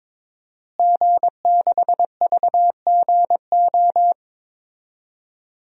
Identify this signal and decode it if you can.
Morse code, heard as a keyed tone, about 22 wpm, 700 Hz. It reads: G6VGO
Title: DXCC-100 Call Signs - 5 Characters - Repeated 2x - 22wpm